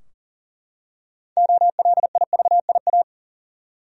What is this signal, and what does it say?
Morse code, heard as a keyed tone, about 40 wpm, 700 Hz. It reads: OLIVIA
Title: USA Common Names - 40wpm